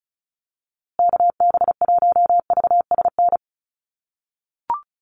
Morse code, heard as a keyed tone, about 34 wpm, 700 Hz. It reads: KB1VSN
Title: USA Call Signs Repeated 2x 35wpm